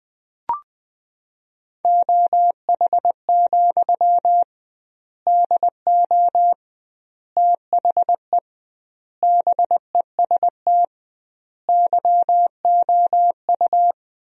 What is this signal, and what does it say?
Morse code, heard as a keyed tone, about 20 wpm, 700 Hz. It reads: OH, DO THE BEST YOU
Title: 2x Word-Spacing - Sentences from Top 500 Words - 20wpm